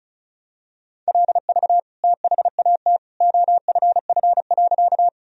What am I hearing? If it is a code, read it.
Morse code, heard as a keyed tone, about 36 wpm, 700 Hz. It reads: LV THAT OFF.
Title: Sentences from Top 400 Words and Abbreviations 35wpm